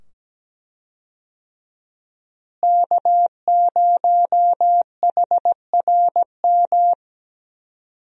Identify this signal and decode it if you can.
Morse code, heard as a keyed tone, about 17 wpm, 700 Hz. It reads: K0HRM
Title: USA Call Signs - 5 Characters - Repeated 2x - 17wpm